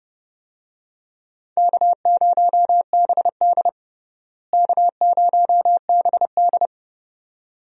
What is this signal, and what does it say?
Morse code, heard as a keyed tone, about 32 wpm, 700 Hz. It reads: K0BD K0BD
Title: USA Call Signs - 4 Characters - Repeated 2x - 30wpm